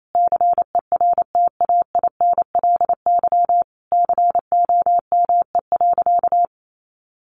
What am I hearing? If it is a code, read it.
Morse code, heard as a keyed tone, about 28 wpm, 700 Hz. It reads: CERTAINLY COME.
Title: Sentences from Top 900 Words 28wpm